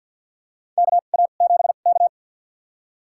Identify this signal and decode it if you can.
Morse code, heard as a keyed tone, about 51 wpm, spaced as for 36 wpm, 700 Hz. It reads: KA7K
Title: USA Call Signs - 4 Characters - 2x Character Spacing - 50wpm